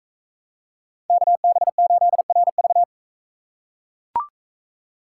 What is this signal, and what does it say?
Morse code, heard as a keyed tone, about 40 wpm, 700 Hz. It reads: KB8RV